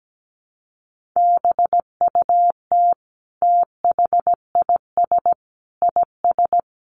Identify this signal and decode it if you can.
Morse code, heard as a keyed tone, about 17 wpm, 700 Hz. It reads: BUT THIS IS
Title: Sentences from Top 1400 Words 17wpm